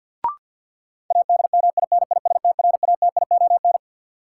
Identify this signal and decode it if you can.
Morse code, heard as a keyed tone, about 53 wpm, 700 Hz. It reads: ADMINISTRATION
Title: Sets of 5 Words - New Word Drill - 50wpm